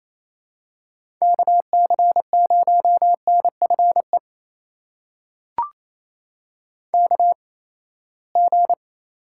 Morse code, heard as a keyed tone, about 28 wpm, spaced as for 19 wpm, 700 Hz. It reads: KC0NFE KG
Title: USA Call Signs - 6 Characters - 8x Character Spacing - 28wpm